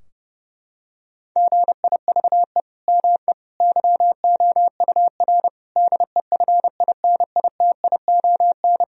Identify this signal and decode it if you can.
Morse code, heard as a keyed tone, about 32 wpm, 700 Hz. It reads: GIVE ME YOUR DEFINITION
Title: Sentences from Top 1000 Words - Review Part 4 - 30wpm